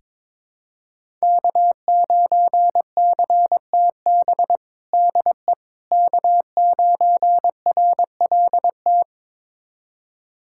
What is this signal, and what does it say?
Morse code, heard as a keyed tone, about 22 wpm, 700 Hz. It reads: K9CTB DE K9RLT
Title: USA Callsign DE Callsign - 5 Characters - 22wpm